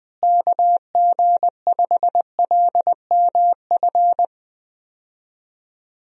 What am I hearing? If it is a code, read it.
Morse code, heard as a keyed tone, about 20 wpm, 700 Hz. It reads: KG5LMF
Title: USA Call Signs - 6 Characters - Repeated 2x - 20wpm